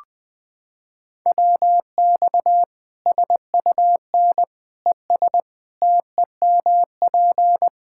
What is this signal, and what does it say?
Morse code, heard as a keyed tone, about 20 wpm, 700 Hz. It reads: WX SUN ES TEMP
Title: K1USN SST QSO Practice 20wpm